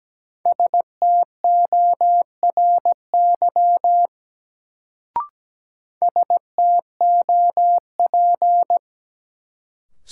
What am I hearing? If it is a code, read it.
Morse code, heard as a keyed tone, about 17 wpm, 700 Hz. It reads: STORY STOP
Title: Sentences from Top 300 Words - New Word Drill - 17wpm